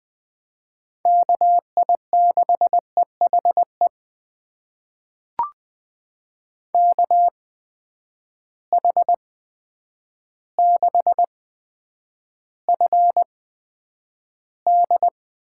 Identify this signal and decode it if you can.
Morse code, heard as a keyed tone, about 20 wpm, spaced as for 9 wpm, 700 Hz. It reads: KI6EHE KH6FD
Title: USA Call Signs - 8x Character Spacing - 20wpm